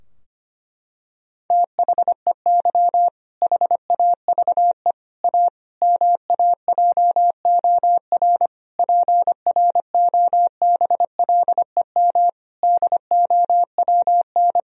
Morse code, heard as a keyed tone, about 25 wpm, 700 Hz. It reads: THEY HAVE A MAJOR PROBLEM DOWN
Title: Sentences from Top 1000 Words - Review Part 4 - 25wpm